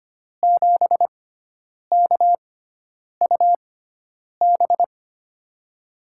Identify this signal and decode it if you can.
Morse code, heard as a keyed tone, about 25 wpm, spaced as for 9 wpm, 700 Hz. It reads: 7KUB
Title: USA Call Signs - 6x Character Spacing - 25wpm